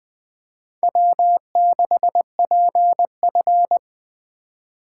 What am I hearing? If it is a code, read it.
Morse code, heard as a keyed tone, about 20 wpm, 700 Hz. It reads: W6PF
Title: Speed Racing - USA 1x2 Call Signs 17wpm